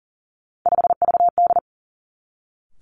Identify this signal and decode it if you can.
Morse code, heard as a keyed tone, about 42 wpm, 700 Hz. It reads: 5VD
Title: Three Challenging Letter-Number Combinations 40wpm